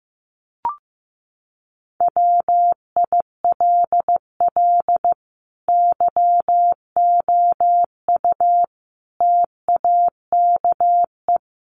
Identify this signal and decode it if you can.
Morse code, heard as a keyed tone, about 15 wpm, 700 Hz. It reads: WILL YOU TAKE
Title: Sentences from Top 1300 Words 15wpm